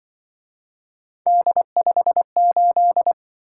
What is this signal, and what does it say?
Morse code, heard as a keyed tone, about 24 wpm, 700 Hz. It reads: D58
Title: Speed-Racing - Three Letter-Number Combinations - 20wpm